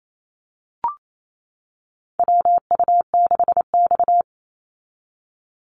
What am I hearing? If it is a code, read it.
Morse code, heard as a keyed tone, about 28 wpm, 700 Hz. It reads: WU6X
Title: Call Signs - 28wpm